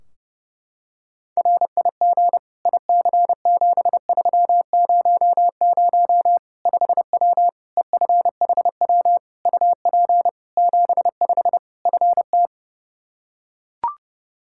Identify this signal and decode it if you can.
Morse code, heard as a keyed tone, about 32 wpm, 700 Hz. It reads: RIG IC7300 5W EFHW UP 75 FT
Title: Practice QSO - Exchange 2 - Info - 30wpm